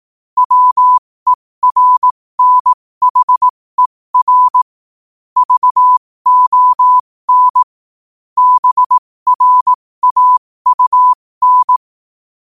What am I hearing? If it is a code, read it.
Morse code, heard as a keyed tone, about 18 wpm, spaced as for 15 wpm, 1000 Hz. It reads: WERNHER VON BRAUN
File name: qotdfiles / 15wpm